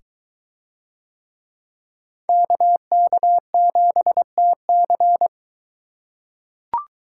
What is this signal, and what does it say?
Morse code, heard as a keyed tone, about 22 wpm, 700 Hz. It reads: KK7TC